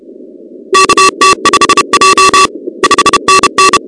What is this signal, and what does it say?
Morse code, heard as a keyed tone, about 30 wpm, 1150 Hz, with noise. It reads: KT5J 5NN
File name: contests / 2019-cw-wpx-kt5j-k5pi- -2330 0-38